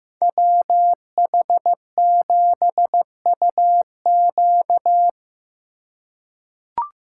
Morse code, heard as a keyed tone, about 15 wpm, 700 Hz. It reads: WH7UQ
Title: USA Call Signs - 5 Characters - 2x Character Spacing - 15wpm